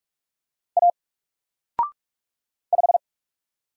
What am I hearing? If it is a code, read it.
Morse code, heard as a keyed tone, about 46 wpm, 700 Hz. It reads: A 5
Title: Single Letter-Number 45wpm